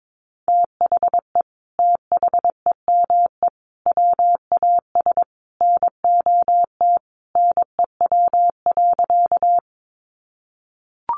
Morse code, heard as a keyed tone, about 22 wpm, 700 Hz. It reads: THE THEME WAS NOT NEW.
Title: Sentences from Top 1100 Words 22wpm